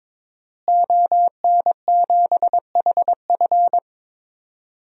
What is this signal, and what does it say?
Morse code, heard as a keyed tone, about 22 wpm, 700 Hz. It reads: ON7HF
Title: DXCC-100 Call Signs - 5 Characters - Repeated 3x - 22wpm